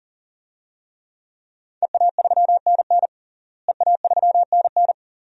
Morse code, heard as a keyed tone, about 40 wpm, 700 Hz. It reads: EA3NN EA3NN
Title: DXCC-100 Call Signs - 5 Characters - Repeated 2x - 40wpm